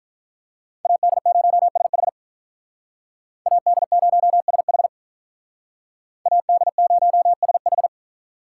Morse code, heard as a keyed tone, about 44 wpm, 700 Hz. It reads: AD0SH AD0SH AD0SH
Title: Speed Racing - USA Call Signs 35wpm